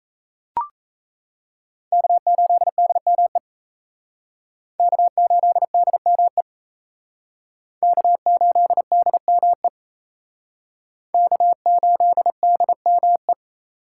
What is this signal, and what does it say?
Morse code, heard as a keyed tone, about 36 wpm, 700 Hz. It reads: K8DME K8DME K8DME K8DME